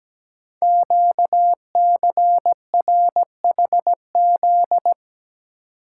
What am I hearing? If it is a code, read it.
Morse code, heard as a keyed tone, about 17 wpm, 700 Hz. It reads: QCRHZ
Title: Code Groups - 5 Letters - 17wpm